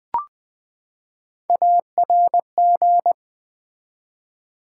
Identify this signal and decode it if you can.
Morse code, heard as a keyed tone, about 20 wpm, 700 Hz. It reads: ARG
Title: Build A Word - 3 Letters At A Time - 8-Letter Words - 20wpm